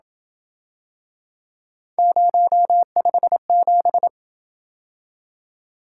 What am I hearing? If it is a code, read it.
Morse code, heard as a keyed tone, about 28 wpm, 700 Hz. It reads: 057